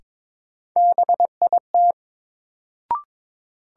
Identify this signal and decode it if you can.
Morse code, heard as a keyed tone, about 22 wpm, 700 Hz. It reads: BIT
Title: Sets of 3 Words - New Word Drill - 22wpm